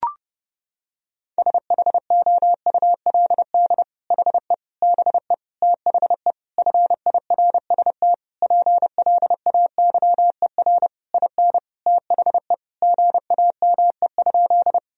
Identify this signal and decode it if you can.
Morse code, heard as a keyed tone, about 30 wpm, 700 Hz. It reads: SHOULD HE BE THE FIRST PLAYER IN THE GAME?
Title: Sentences from Top 1000 Words - Review Part 4 - 30wpm